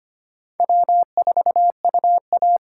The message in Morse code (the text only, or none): W4UA